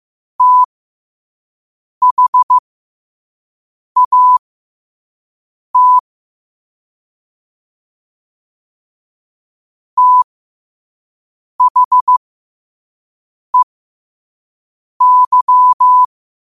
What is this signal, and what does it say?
Morse code, heard as a keyed tone, about 15 wpm, spaced as for 5 wpm, 1000 Hz. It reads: THAT THEY
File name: 5wpm